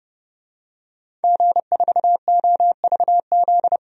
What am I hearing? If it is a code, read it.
Morse code, heard as a keyed tone, about 30 wpm, 690 Hz. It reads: G4OVZ
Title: DXCC-100 Call Signs - 5 Characters - 30wpm